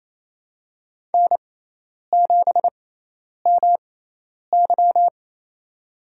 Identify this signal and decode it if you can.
Morse code, heard as a keyed tone, about 28 wpm, spaced as for 10 wpm, 700 Hz. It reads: N7MY
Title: USA Call Signs - 4 Characters - 6x Character Spacing - 28wpm